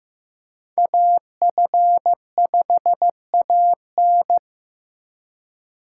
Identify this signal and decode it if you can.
Morse code, heard as a keyed tone, about 16 wpm, 700 Hz. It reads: AF5AN
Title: USA Call Signs - 5 Characters - 4x Character Spacing - 15wpm